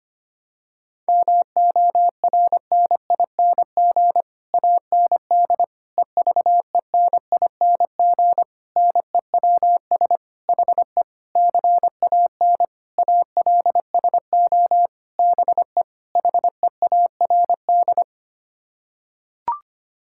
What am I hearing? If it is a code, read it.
Morse code, heard as a keyed tone, about 25 wpm, 700 Hz. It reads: MORNING AND EVENING NEWS HE CAN ALSO BE HEARD
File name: Sets of 9 Words - 25wpm